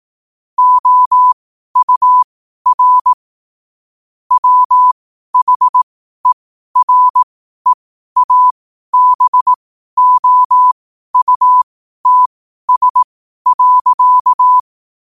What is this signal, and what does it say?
Morse code, heard as a keyed tone, about 18 wpm, spaced as for 12 wpm, 1000 Hz. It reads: OUR WHEREABOUTS.
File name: qotdfiles / 12wpm